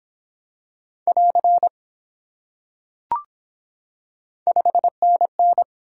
Morse code, heard as a keyed tone, about 24 wpm, 700 Hz. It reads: <AR> 5NN